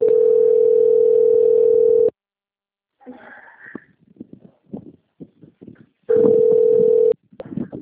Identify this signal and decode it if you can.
Ringback tone (North America), 440 + 480 Hz.